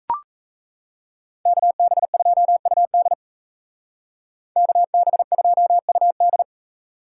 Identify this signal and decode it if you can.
Morse code, heard as a keyed tone, about 36 wpm, 700 Hz. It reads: KB2UD KB2UD